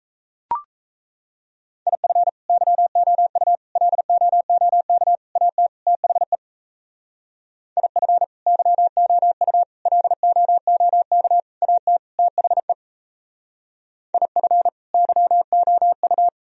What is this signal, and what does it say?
Morse code, heard as a keyed tone, about 37 wpm, 700 Hz. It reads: IF YOU LOOK AT THE IF YOU LOOK AT THE IF YOU